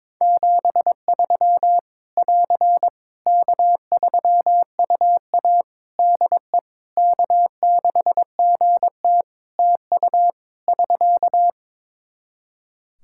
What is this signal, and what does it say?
Morse code, heard as a keyed tone, about 22 wpm, 700 Hz. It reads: 73 <AR> K3UA DE K6GT TU <SK>
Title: Practice QSO - Exchange 4 - Ending - 22wpm